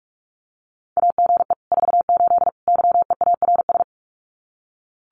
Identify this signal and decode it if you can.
Morse code, heard as a keyed tone, about 44 wpm, 700 Hz. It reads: AGE 48 YEARS
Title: Common QSO Phrases Encore 45wpm